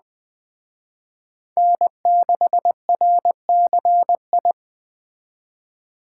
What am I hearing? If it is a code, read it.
Morse code, heard as a keyed tone, about 21 wpm, 700 Hz. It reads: N6RCI
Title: Speed Racing - USA 1x3 Call Signs 17wpm